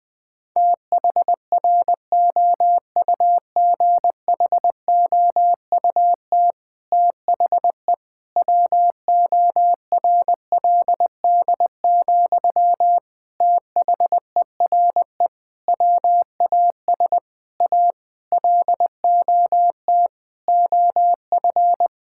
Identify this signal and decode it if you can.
Morse code, heard as a keyed tone, about 20 wpm, 700 Hz. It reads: THROUGHOUT THE WORLD, THERE WAS A LOT OF
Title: Sentences from Top 1000 Words - Review Part 5 - 20wpm